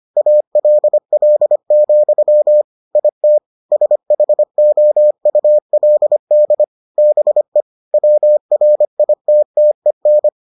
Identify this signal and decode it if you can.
Morse code, heard as a keyed tone, about 25 wpm, 600 Hz. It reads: ALL, IT SHOULD BE WRITTEN